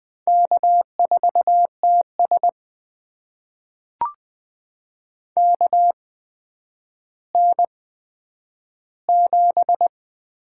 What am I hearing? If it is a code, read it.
Morse code, heard as a keyed tone, about 20 wpm, spaced as for 10 wpm, 700 Hz. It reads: K4TS KN7
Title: USA Call Signs - 8x Character Spacing - 20wpm